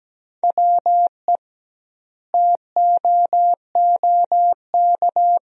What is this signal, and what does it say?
Morse code, heard as a keyed tone, about 17 wpm, 700 Hz. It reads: WE TOOK
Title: 2x Word-Spacing - Sentences from Top 400 Words - 17wpm